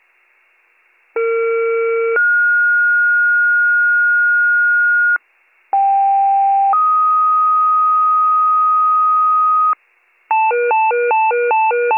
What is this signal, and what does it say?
Two-tone page, heard as a one-second tone, then a three-second tone, about 460 then 1430 Hz, about 790 then 1250 Hz, with noise. It is heard from 1.2 to 9.7 s.